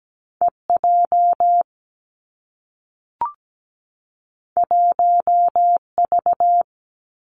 Morse code, heard as a keyed tone, about 17 wpm, 700 Hz. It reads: E J 1 V
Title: Two Challenging Letter-Number Combinations 17wpm